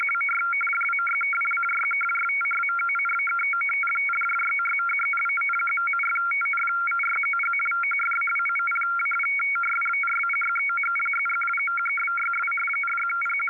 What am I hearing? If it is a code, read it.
Two-tone teleprinter signal, about 850 Hz shift.
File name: SharedFiles / AudTfer / FSK example